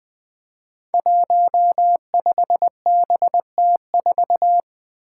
Morse code, heard as a keyed tone, about 20 wpm, 700 Hz. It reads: 15BT4
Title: Five Challenging Letter-Number Combinations 20wpm